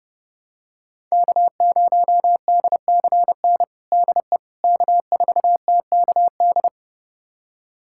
Morse code, heard as a keyed tone, about 30 wpm, 700 Hz. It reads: K0DCN DE K4TKD
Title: USA Callsign DE Callsign - 5 Characters - 30wpm